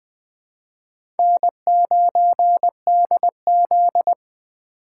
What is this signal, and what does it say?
Morse code, heard as a keyed tone, about 20 wpm, 700 Hz. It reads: N9DZ
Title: USA Call Signs - 4 Characters - 4x Character Spacing - 20wpm